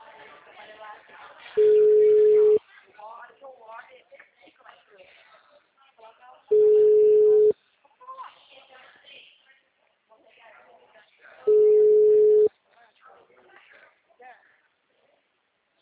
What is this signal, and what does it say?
Ringback tone (Europe, 425 Hz), with speech.